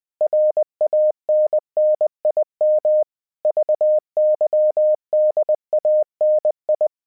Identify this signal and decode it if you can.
Morse code, heard as a keyed tone, about 20 wpm, 610 Hz. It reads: RANNIM VYDANI
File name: 20wpm